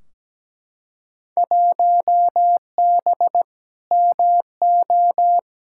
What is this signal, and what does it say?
Morse code, heard as a keyed tone, about 17 wpm, 700 Hz. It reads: 1B MO
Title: Field Day Exchanges - 17wpm